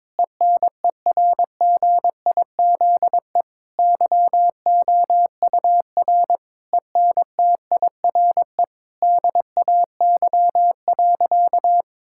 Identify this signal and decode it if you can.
Morse code, heard as a keyed tone, about 22 wpm, 700 Hz. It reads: ENERGIZE YOUR ENTIRE DAY.
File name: Difficult Sentences with Hints - 22wpm